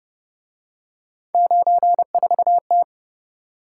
Morse code, heard as a keyed tone, about 29 wpm, 700 Hz. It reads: 94T